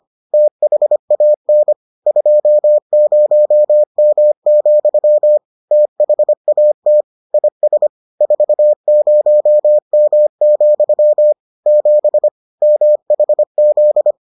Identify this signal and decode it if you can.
Morse code, heard as a keyed tone, about 25 wpm, 600 Hz. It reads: THAN 20M, THAT IS 40M, 7 MHZ